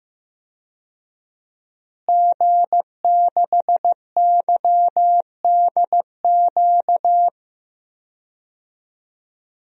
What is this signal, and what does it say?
Morse code, heard as a keyed tone, about 15 wpm, 700 Hz. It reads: G6YDQ
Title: DXCC-100 Call Signs - 5 Characters - Repeated 2x - 15wpm